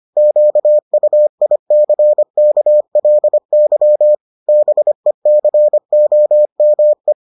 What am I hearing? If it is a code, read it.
Morse code, heard as a keyed tone, about 25 wpm, 600 Hz. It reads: QUICKLY BECOME